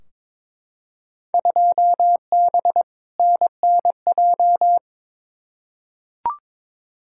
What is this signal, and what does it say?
Morse code, heard as a keyed tone, about 22 wpm, 700 Hz. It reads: 2B NNJ
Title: Field Day Exchanges Encore - 22wpm